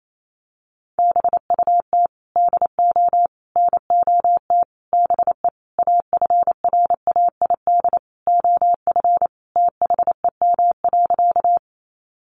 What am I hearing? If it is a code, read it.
Morse code, heard as a keyed tone, about 28 wpm, 700 Hz. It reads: BUT DO NOT BE AFRAID OF THEM.
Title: Sentences from Top 1500 Words 28wpm